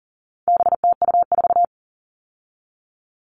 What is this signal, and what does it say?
Morse code, heard as a keyed tone, about 40 wpm, 700 Hz. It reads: BTU4
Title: All Cut Numbers - 4-Digit 40wpm